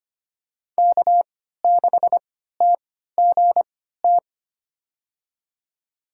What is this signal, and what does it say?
Morse code, heard as a keyed tone, about 25 wpm, spaced as for 14 wpm, 700 Hz. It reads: K6TGT
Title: USA Call Signs - 3x Character Spacing - 25wpm